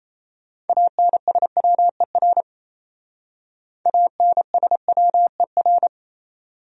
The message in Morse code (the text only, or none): ANSWER ANSWER